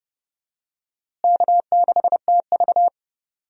Morse code, heard as a keyed tone, about 30 wpm, 700 Hz. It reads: K6TV
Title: USA Call Signs - 4 Characters - Repeated 3x - 30wpm